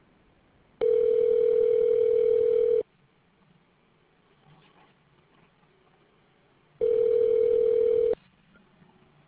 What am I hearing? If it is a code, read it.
Ringback tone (North America), 440 + 480 Hz.